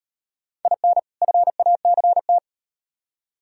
Morse code, heard as a keyed tone, about 38 wpm, 700 Hz. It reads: IN FACT